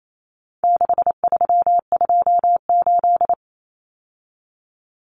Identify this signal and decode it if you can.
Morse code, heard as a keyed tone, about 28 wpm, 700 Hz. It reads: 6328
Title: Common Cut Numbers - 4-Digit 28wpm